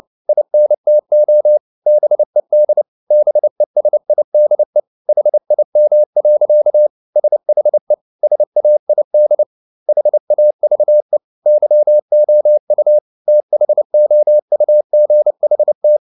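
Morse code, heard as a keyed tone, about 29 wpm, 600 Hz. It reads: INTO BED BESIDE HIM. SHE SAID HAVE YOU THOUGHT
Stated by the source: Morse code at 29 wpm, sending Ken Follett's novel Never in English, from CW med Gnister.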